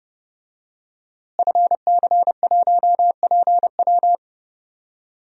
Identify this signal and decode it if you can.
Morse code, heard as a keyed tone, about 30 wpm, 700 Hz. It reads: FC1PW
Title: DXCC-100 Call Signs - 5 Characters - Repeated 3x - 30wpm